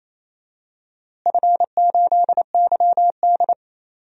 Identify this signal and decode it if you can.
Morse code, heard as a keyed tone, about 28 wpm, 700 Hz. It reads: F8YD
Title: DXCC-100 Call Signs - 4 Characters - Repeated 2x - 28wpm